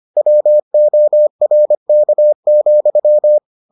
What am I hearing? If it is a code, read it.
Morse code, heard as a keyed tone, about 25 wpm, 600 Hz. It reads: WORK,